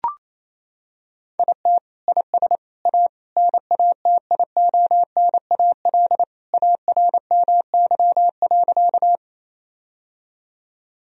Morse code, heard as a keyed tone, about 28 wpm, 700 Hz. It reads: IT IS A NATIONAL ARMY.
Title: Sentences from Top 1000 Words - Review Part 4 - 28wpm